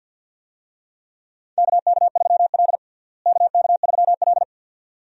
Morse code, heard as a keyed tone, about 51 wpm, 700 Hz. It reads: KK3L KK3L
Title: USA Call Signs - 4 Characters - Repeated 2x - 50wpm